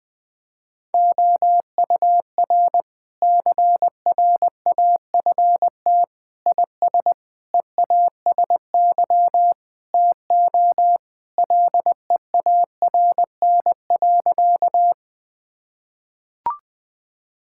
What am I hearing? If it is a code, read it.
Morse code, heard as a keyed tone, about 20 wpm, 700 Hz. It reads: OUR CRAFT IS EASY TO LEARN.
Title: Sentences from Top 1000 Words - Review Part 6 - 20wpm